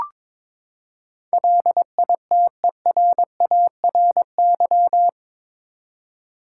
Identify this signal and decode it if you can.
Morse code, heard as a keyed tone, about 22 wpm, 700 Hz. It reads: LITERARY
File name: practice / 22wpm / Eight Letter Words - 22wpm